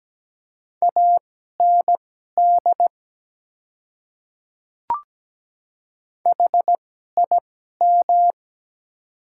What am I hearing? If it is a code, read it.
Morse code, heard as a keyed tone, about 17 wpm, spaced as for 12 wpm, 700 Hz. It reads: AND HIM
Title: 2x Character Spacing - Top 200 Words - 17wpm